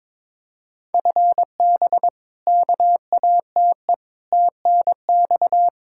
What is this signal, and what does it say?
Morse code, heard as a keyed tone, about 21 wpm, 700 Hz. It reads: FB KATE TNX
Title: Practice QSO - Exchange 3 - More Info - 22wpm